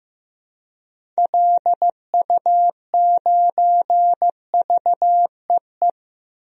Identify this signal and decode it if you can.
Morse code, heard as a keyed tone, about 15 wpm, 700 Hz. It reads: LU9VEE
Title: DXCC-100 Call Signs - 6 Characters - 15wpm